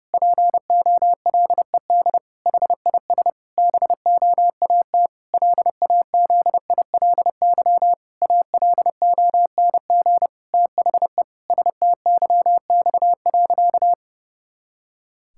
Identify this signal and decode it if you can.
Morse code, heard as a keyed tone, about 32 wpm, 700 Hz. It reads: POLED HIS BOAT LAZILY ALONG THE STYX.
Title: Pangram Challenge v2 - 30wpm